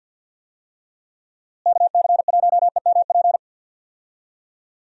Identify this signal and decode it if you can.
Morse code, heard as a keyed tone, about 49 wpm, 700 Hz. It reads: KC1EMP
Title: USA Call Signs - 6 Characters - Repeated 2x - 50wpm